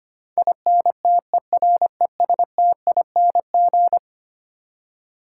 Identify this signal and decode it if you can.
Morse code, heard as a keyed tone, about 25 wpm, 700 Hz. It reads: INTERESTING
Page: morse-code-ninja-practice-25wpm